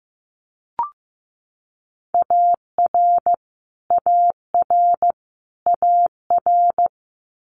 Morse code, heard as a keyed tone, about 15 wpm, 700 Hz. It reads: AR AR AR
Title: Multi-letter Phonemes Repeated 3x 15wpm